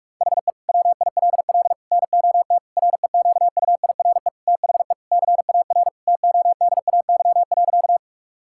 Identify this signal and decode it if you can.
Morse code, heard as a keyed tone, about 41 wpm, 700 Hz. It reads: HE WILL NOT REQUIRE THE CAR TODAY.
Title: Sentences from Top 1000 Words - Review Part 6 - 45wpm